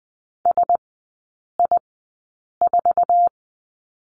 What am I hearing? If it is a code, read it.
Morse code, heard as a keyed tone, about 20 wpm, spaced as for 8 wpm, 700 Hz. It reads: SI4
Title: 145 Lesson 24e - Letter y - Three Characters 20wpm